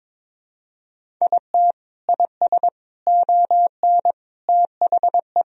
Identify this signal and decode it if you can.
Morse code, heard as a keyed tone, about 21 wpm, 700 Hz. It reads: IT IS ON THE
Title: Sentences from Top 1000 Words - Review Part 1 - 22wpm